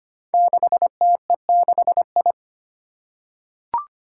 Morse code, heard as a keyed tone, about 25 wpm, 700 Hz. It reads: 6TE6I
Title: Five Challenging Letter-Number Combinations 25wpm